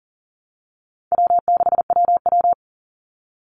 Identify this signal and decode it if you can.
Morse code, heard as a keyed tone, about 40 wpm, 700 Hz. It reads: W6WW
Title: NA SOTA Chaser Callsigns 40wpm